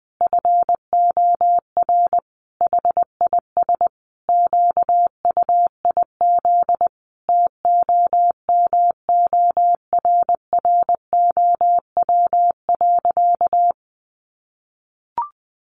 Morse code, heard as a keyed tone, about 20 wpm, 700 Hz. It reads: FOR HIS QUIZ TOMORROW.
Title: Additional Easy Phrases 20wpm